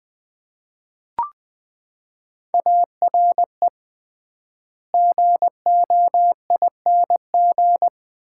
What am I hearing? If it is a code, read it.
Morse code, heard as a keyed tone, about 21 wpm, 700 Hz. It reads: ARE GOING
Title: 3x Word-Spacing - Sets of 4 Words - 20wpm